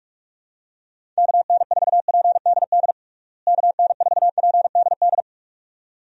Morse code, heard as a keyed tone, about 46 wpm, 700 Hz. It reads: KN4PDD KN4PDD
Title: USA Call Signs - 6 Characters - Repeated 2x - 45wpm